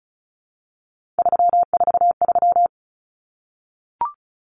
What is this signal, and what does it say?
Morse code, heard as a keyed tone, about 34 wpm, 700 Hz. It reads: 343
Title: Common Cut Numbers - 3-Digit 35wpm